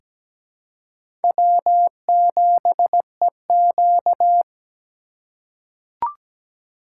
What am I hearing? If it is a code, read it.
Morse code, heard as a keyed tone, about 17 wpm, 700 Hz. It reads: W7EQ
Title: USA Call Signs - 4x Character Spacing - 17wpm